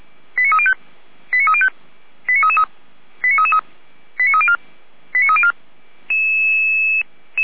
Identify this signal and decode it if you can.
Five-tone selective call (ZVEI), heard as quick runs of five tones, tones of about 70 ms. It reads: 89305 89305 89333 89333 89304 89304
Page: SDR/ZVEI